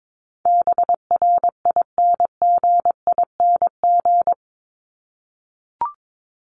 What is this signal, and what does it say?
Morse code, heard as a keyed tone, about 22 wpm, 700 Hz. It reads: BRINGING
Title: ING Suffix - Top 100 Words with Base Words 22wpm